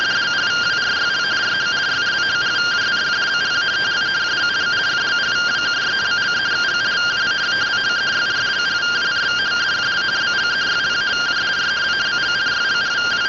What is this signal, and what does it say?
Two-tone teleprinter signal, about 170 Hz shift.